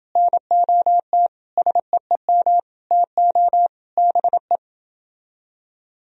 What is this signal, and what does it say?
Morse code, heard as a keyed tone, about 29 wpm, 700 Hz. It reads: NOT SEEM TO BE